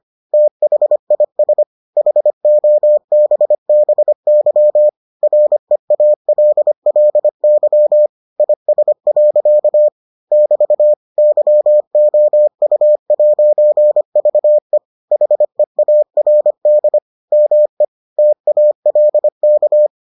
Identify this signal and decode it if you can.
Morse code, heard as a keyed tone, about 25 wpm, 600 Hz. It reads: THIS HOBBY REALLY IS. <BT> YOU'VE HEARD ME TALK